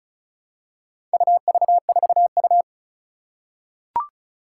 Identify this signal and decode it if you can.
Morse code, heard as a keyed tone, about 34 wpm, 700 Hz. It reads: UV4U
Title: Four Challenging Letter-Number Combinations 35wpm